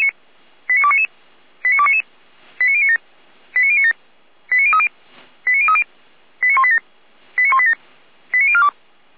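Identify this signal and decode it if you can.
Five-tone selective call (ZVEI), heard as quick runs of five tones, tones of about 70 ms. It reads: 89299 89299 89097 89097 89030 89030 89187 89187 89042
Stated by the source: ZVEI five-tone selective call received on an SDR.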